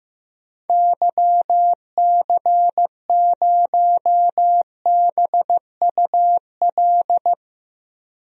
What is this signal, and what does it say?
Morse code, heard as a keyed tone, about 15 wpm, 700 Hz. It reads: YC0BUL
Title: DXCC-100 Call Signs - 6 Characters - Repeated 3x - 15wpm